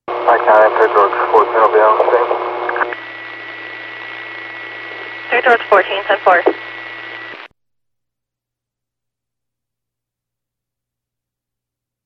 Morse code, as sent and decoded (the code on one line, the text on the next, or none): none